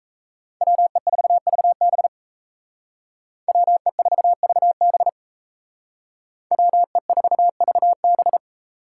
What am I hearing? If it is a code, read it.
Morse code, heard as a keyed tone, about 37 wpm, 700 Hz. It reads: WE4VB WE4VB WE4VB